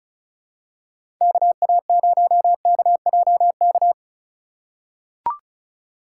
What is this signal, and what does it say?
Morse code, heard as a keyed tone, about 36 wpm, 700 Hz. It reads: KA0KJK
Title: USA Call Signs - 6 Characters - 3x Character Spacing - 35wpm